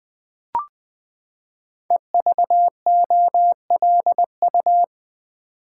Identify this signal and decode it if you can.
Morse code, heard as a keyed tone, about 20 wpm, 700 Hz. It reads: EVOLU
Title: Build A Word - 5 Letters At A Time - 12-Letter Words - 20wpm